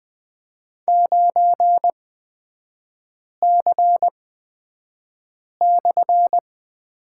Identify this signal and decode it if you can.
Morse code, heard as a keyed tone, about 20 wpm, spaced as for 5 wpm, 690 Hz. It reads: 9C/
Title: 229 Lesson 37e - Letter x - Three Characters 20wpm